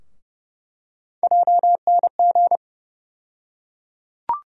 Morse code, heard as a keyed tone, about 30 wpm, 700 Hz. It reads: JNG
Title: USA Call Sign Postfixes - 3 Letters 30wpm